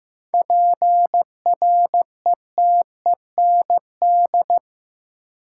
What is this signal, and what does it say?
Morse code, heard as a keyed tone, about 15 wpm, 700 Hz. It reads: PRETEND
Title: Seven Letter Words - 15wpm